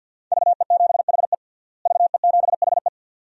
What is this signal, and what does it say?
Morse code, heard as a keyed tone, about 46 wpm, 700 Hz. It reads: VE7HE VE7HE
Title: DXCC-100 Call Signs - 5 Characters - Repeated 2x - 50wpm